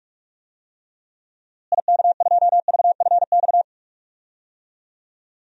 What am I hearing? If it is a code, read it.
Morse code, heard as a keyed tone, about 46 wpm, 700 Hz. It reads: IK2VFX
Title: DXCC-100 Call Signs - 6 Characters - Repeated 2x - 45wpm